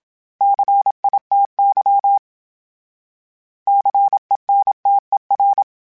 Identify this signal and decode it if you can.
Morse code, heard as a keyed tone, about 27 wpm, 800 Hz. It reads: CITY CENTER
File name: fromSMs / FromSMs mp3s / Cities@25